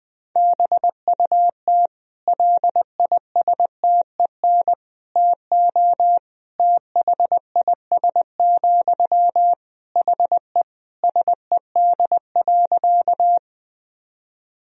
Morse code, heard as a keyed tone, about 21 wpm, 700 Hz. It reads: BUT LISTEN TO THIS, HE SED.
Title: Sentences from Top 400 Words and Abbreviations 20wpm